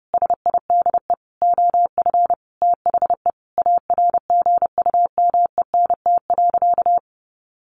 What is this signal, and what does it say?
Morse code, heard as a keyed tone, about 30 wpm, 700 Hz. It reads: SIDE OF THE ARGUMENT.
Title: Sentences from Top 1400 Words 30wpm